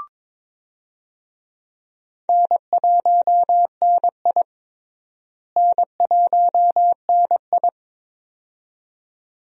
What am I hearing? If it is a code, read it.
Morse code, heard as a keyed tone, about 22 wpm, 700 Hz. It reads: N1NI N1NI
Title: USA Call Signs - 4 Characters - Repeated 2x - 22wpm